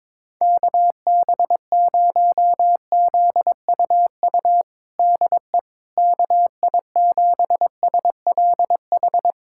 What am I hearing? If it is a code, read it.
Morse code, heard as a keyed tone, about 22 wpm, 700 Hz. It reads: KB0ZUU DE KI7SLH
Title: USA Callsign DE Callsign - 6 Characters - 22wpm